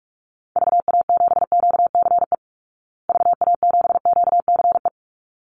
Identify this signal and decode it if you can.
Morse code, heard as a keyed tone, about 44 wpm, 700 Hz. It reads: VA7QCE VA7QCE
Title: Call Signs Repeated 3x - 45wpm